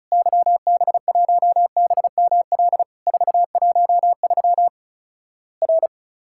Morse code, heard as a keyed tone, about 34 wpm, 700 Hz. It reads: YB1BML 413 R
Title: Worked All Europe DX Contest - QTC Exchange - 35wpm